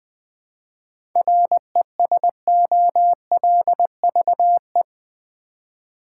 Morse code, heard as a keyed tone, about 21 wpm, 700 Hz. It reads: RESOLVE
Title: Seven Letter Words - 20wpm